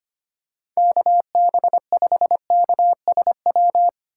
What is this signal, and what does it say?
Morse code, heard as a keyed tone, about 25 wpm, 700 Hz. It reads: KB5KSW
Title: USA Call Signs - 2x Character Spacing - 25wpm